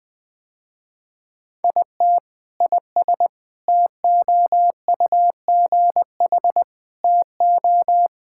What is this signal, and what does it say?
Morse code, heard as a keyed tone, about 21 wpm, 700 Hz. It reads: IT IS TOUGH TO
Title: Sentences with 6-Word Phrases - Spoken Beginning - 20wpm